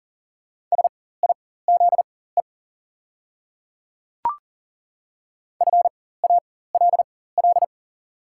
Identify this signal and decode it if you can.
Morse code, heard as a keyed tone, about 40 wpm, spaced as for 19 wpm, 700 Hz. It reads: SIZE FALL
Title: 4x Character Spacing - Top 500 Words - 40wpm